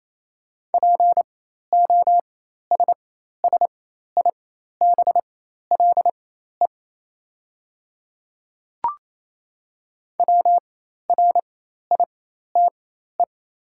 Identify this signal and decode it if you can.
Morse code, heard as a keyed tone, about 28 wpm, spaced as for 13 wpm, 690 Hz. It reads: POSSIBLE WRITE
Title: 4x Character Spacing - Top 500 Words - 28wpm